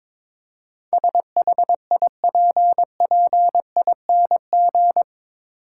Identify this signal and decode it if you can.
Morse code, heard as a keyed tone, about 22 wpm, 700 Hz. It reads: SHIPPING
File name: practice / 22wpm / Eight Letter Words - 22wpm